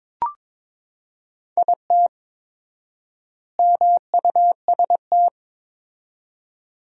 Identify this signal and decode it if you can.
Morse code, heard as a keyed tone, about 22 wpm, 700 Hz. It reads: IT MUST
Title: 4x Word-Spacing - Sentences from Top 500 Words - 22wpm